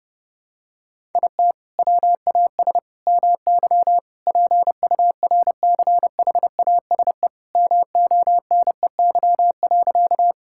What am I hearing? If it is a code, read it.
Morse code, heard as a keyed tone, about 30 wpm, 700 Hz. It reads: IT WAS MY PURCHASE MONEY.
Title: Sentences from Top 1000 Words - Review Part 5 - 30wpm